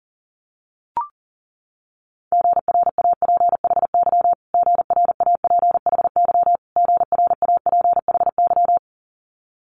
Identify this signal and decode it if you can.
Morse code, heard as a keyed tone, about 40 wpm, 700 Hz. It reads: GRAPHY GRAPHY GRAPHY
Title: Multi-letter Phonemes Repeated 3x 40wpm